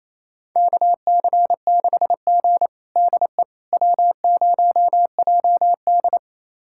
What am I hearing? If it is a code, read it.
Morse code, heard as a keyed tone, about 28 wpm, 700 Hz. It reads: KC6G DE W0JD
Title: USA Callsign DE Callsign - 4 Characters - 28wpm